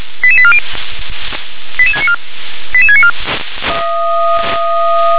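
Five-tone selective call (ZVEI), heard as quick runs of five tones, tones of about 70 ms. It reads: 88840 88694 88694
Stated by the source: ZVEI five-tone selective call 88765 received by SDR.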